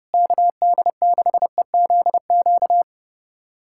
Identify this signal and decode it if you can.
Morse code, heard as a keyed tone, about 30 wpm, 700 Hz. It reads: KD6EZQ
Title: USA Callsign DE Callsign - 6 Characters - 30wpm